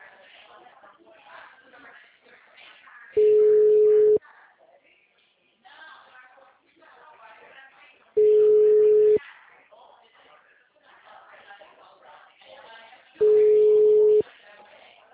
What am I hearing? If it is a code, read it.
Ringback tone (Europe, 425 Hz), with speech.
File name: out- -2027- - - .58394